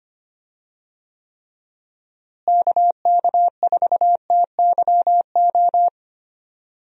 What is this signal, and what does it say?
Morse code, heard as a keyed tone, about 25 wpm, 700 Hz. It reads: KK4TYO
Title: USA Call Signs - 6 Characters - Repeated 3x - 25wpm